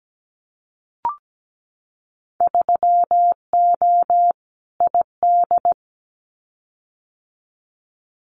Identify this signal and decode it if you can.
Morse code, heard as a keyed tone, about 17 wpm, 700 Hz. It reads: 3O ID
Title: Winter Field Day Exchanges Encore 17wpm